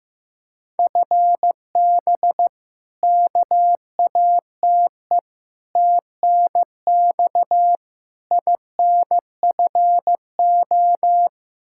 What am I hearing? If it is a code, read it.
Morse code, heard as a keyed tone, about 15 wpm, 700 Hz. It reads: FB KATE TNX INFO
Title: Practice QSO - Exchange 3 - More Info - 15wpm